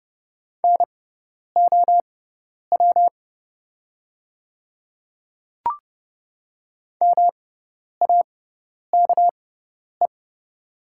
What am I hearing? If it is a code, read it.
Morse code, heard as a keyed tone, about 30 wpm, spaced as for 10 wpm, 700 Hz. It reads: NOW MAKE
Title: 6x Character Spacing - Top 200 Words - 30wpm